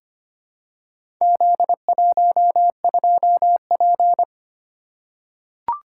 Morse code, heard as a keyed tone, about 26 wpm, 700 Hz. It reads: Z12P
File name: practice / 25wpm / Four Letter-Number Combinations - 25wpm